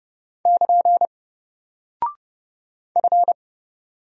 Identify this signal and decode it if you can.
Morse code, heard as a keyed tone, about 30 wpm, 700 Hz. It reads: <KN> F
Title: Single Character (Letter:Number:Punctuation:Prosign) 30wpm